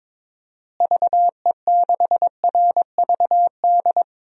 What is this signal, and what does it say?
Morse code, heard as a keyed tone, about 22 wpm, 700 Hz. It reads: VE6RVD
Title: DXCC-100 Call Signs - 6 Characters - Repeated 3x - 22wpm